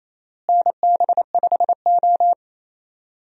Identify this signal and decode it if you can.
Morse code, heard as a keyed tone, about 28 wpm, 700 Hz. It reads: NB5O
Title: USA Call Signs - 4 Characters - Repeated 3x - 28wpm